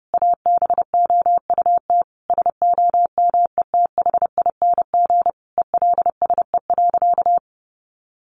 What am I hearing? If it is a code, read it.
Morse code, heard as a keyed tone, about 30 wpm, 700 Hz. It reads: ABOUT SOMETHING ELSE.
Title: Sentences from Top 800 Words 30wpm